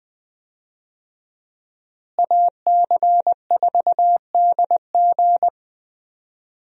Morse code, heard as a keyed tone, about 20 wpm, 700 Hz. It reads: AC4DG
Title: USA Call Signs - 5 Characters - Repeated 3x - 20wpm